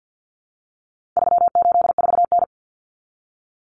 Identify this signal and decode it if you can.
Morse code, heard as a keyed tone, about 48 wpm, 700 Hz. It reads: 384N
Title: Common Cut Numbers - 4-Digit 50wpm